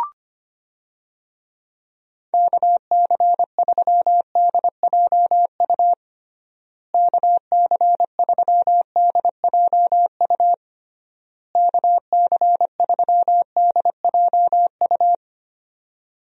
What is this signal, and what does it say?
Morse code, heard as a keyed tone, about 25 wpm, 700 Hz. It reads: KC3DJU KC3DJU KC3DJU
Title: USA Call Signs - 6 Characters - Repeated 3x - 25wpm